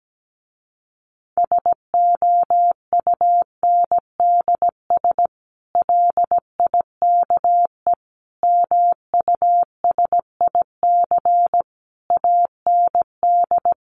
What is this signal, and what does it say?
Morse code, heard as a keyed tone, about 17 wpm, 700 Hz. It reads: SOUNDS LIKE MUSIC AND
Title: A Sound Of Thunder 17wpm